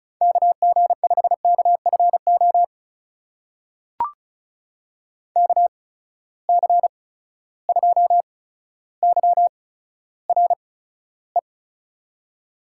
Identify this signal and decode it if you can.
Morse code, heard as a keyed tone, about 36 wpm, spaced as for 15 wpm, 700 Hz. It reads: KG5KFO KC2YRE
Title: USA Call Signs - 6 Characters - 8x Character Spacing - 35wpm